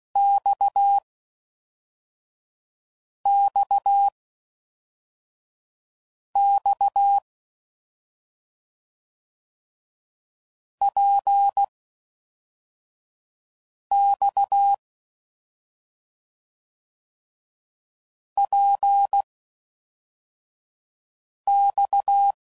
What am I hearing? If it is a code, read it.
Morse code, heard as a keyed tone, about 16 wpm, spaced as for 4 wpm, 800 Hz. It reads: XXX PX PX